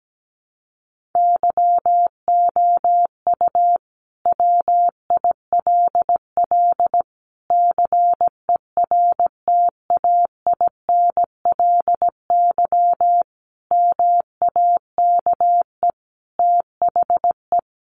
Morse code, heard as a keyed tone, about 17 wpm, 700 Hz. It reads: YOU WILL CERTAINLY MAKE THE
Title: Sentences from Top 1400 Words 17wpm